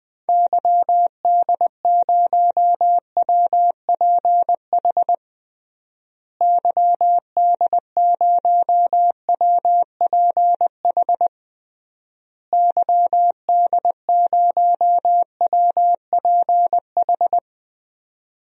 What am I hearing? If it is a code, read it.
Morse code, heard as a keyed tone, about 21 wpm, 700 Hz. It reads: YD0WPH YD0WPH YD0WPH
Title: DXCC-100 Call Signs - 6 Characters - Repeated 3x - 20wpm